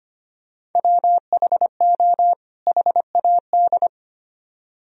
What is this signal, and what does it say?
Morse code, heard as a keyed tone, about 30 wpm, 700 Hz. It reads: WHO HAD
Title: Speed-Racing - Sets of 2 Words 25wpm